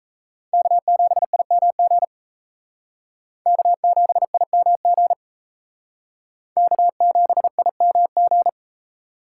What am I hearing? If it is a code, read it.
Morse code, heard as a keyed tone, about 36 wpm, 700 Hz. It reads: K7IMG K7IMG K7IMG